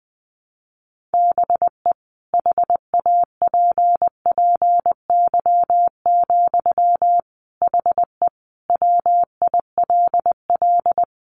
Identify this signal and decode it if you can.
Morse code, heard as a keyed tone, about 19 wpm, 700 Hz. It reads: BE HAPPY, HE WILL
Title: Sentences from Top 1200 Words 20wpm